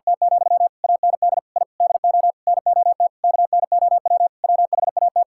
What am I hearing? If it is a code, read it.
Morse code, heard as a keyed tone, about 52 wpm, 700 Hz. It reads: T, AND I DO NOT KNOW WHAT
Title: Sentences from Top 1000 Words - Review Part 5 - 50wpm